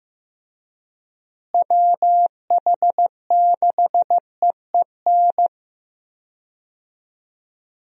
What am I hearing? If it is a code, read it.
Morse code, heard as a keyed tone, about 15 wpm, 700 Hz. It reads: WH6EEN
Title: USA Call Signs - 6 Characters - Repeated 2x - 15wpm